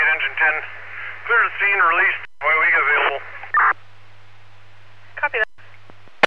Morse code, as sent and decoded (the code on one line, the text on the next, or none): none